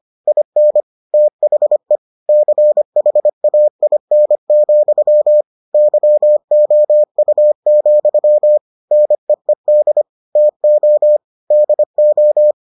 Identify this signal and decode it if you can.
Morse code, heard as a keyed tone, about 25 wpm, 600 Hz. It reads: IN THE CHAIN, YOU, NEED TO DO